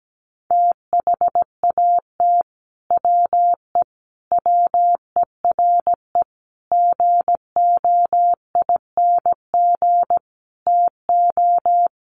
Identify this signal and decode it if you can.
Morse code, heard as a keyed tone, about 17 wpm, 700 Hz. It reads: THAT WE WERE GOING TO
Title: Sets of 5 Words Encore2 17wpm